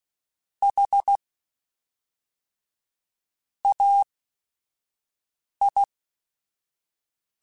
Morse code, heard as a keyed tone, about 16 wpm, spaced as for 5 wpm, 800 Hz. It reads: H AI
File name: Technical Subjects / Technical Subjects-mp3s / XC @5